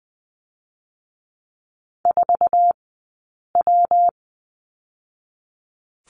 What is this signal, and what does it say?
Morse code, heard as a keyed tone, about 20 wpm, 700 Hz. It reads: 4 W
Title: 132 Lesson 22d - Question Mark - Two Characters 20wpm